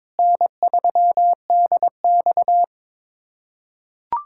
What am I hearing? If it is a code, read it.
Morse code, heard as a keyed tone, about 22 wpm, 700 Hz. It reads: N3DX
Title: USA Callsign DE Callsign - 4 Characters - 22wpm